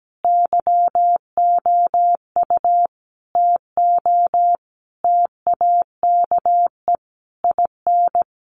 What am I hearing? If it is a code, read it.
Morse code, heard as a keyed tone, about 17 wpm, 700 Hz. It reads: YOU TO TAKE IN
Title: Sentences from Top 1500 Words 17wpm